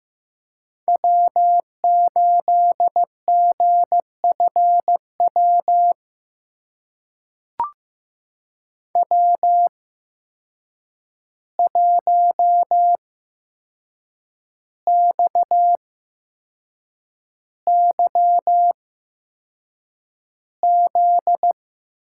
Morse code, heard as a keyed tone, about 15 wpm, spaced as for 6 wpm, 700 Hz. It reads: W8GFW W1XYZ
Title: USA Call Signs - 5 Characters - 8x Character Spacing - 15wpm